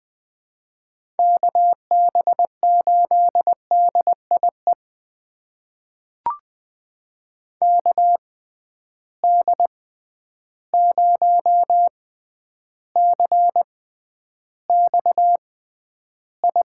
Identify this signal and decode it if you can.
Morse code, heard as a keyed tone, about 20 wpm, spaced as for 10 wpm, 700 Hz. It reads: KB8DIE KD0CXI
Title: USA Call Signs - 6 Characters - 6x Character Spacing - 20wpm